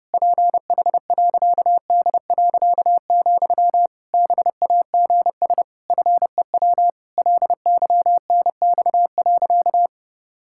Morse code, heard as a keyed tone, about 30 wpm, 700 Hz. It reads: PH.D., BAGS FEW LYNX.
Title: Pangram Challenge v2 - 30wpm